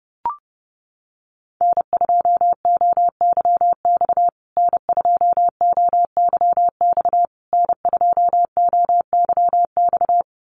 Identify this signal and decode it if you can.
Morse code, heard as a keyed tone, about 30 wpm, 700 Hz. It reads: N2OYX N2OYX N2OYX
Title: USA Call Signs Repeated 3x 30wpm